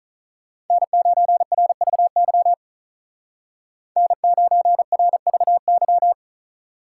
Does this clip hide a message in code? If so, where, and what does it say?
Yes: Morse code from 0.7 to 6.1 s, heard as a keyed tone, about 46 wpm, 700 Hz. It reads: N9RVY N9RVY